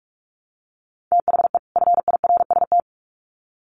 Morse code, heard as a keyed tone, about 44 wpm, 700 Hz. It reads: THE FIRST
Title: Sets of 2 Words 45wpm